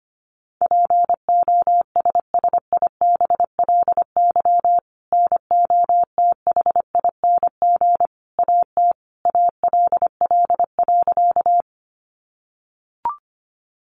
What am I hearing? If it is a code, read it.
Morse code, heard as a keyed tone, about 25 wpm, 700 Hz. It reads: POSSIBLY NOTHING AT ALL.
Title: Sentences from Top 1300 Words 25wpm